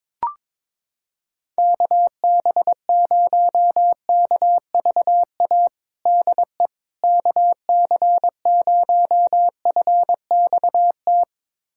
Morse code, heard as a keyed tone, about 22 wpm, 700 Hz. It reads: KB0KVA DE KC0FXT
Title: USA Callsign DE Callsign - 6 Characters - 22wpm